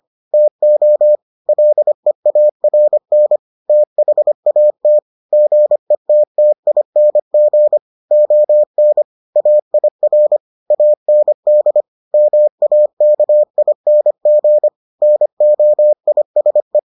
Morse code, heard as a keyed tone, about 25 wpm, 600 Hz. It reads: TO LEARN THAT GETTING ON AIR AND MAKING NOISE